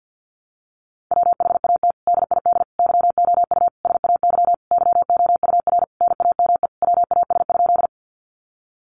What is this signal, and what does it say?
Morse code, heard as a keyed tone, about 50 wpm, 700 Hz. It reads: WHAT DID YOU SAY YOUR NAME WAS?
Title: Sentences from Top 200 Words 50wpm